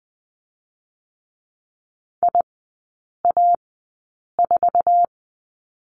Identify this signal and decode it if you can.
Morse code, heard as a keyed tone, about 20 wpm, spaced as for 8 wpm, 690 Hz. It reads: IA4
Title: 087 Lesson 15e - Number 5 - Three Characters 20wpm